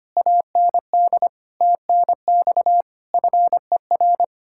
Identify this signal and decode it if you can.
Morse code, heard as a keyed tone, about 32 wpm, 700 Hz. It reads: AND TNX FER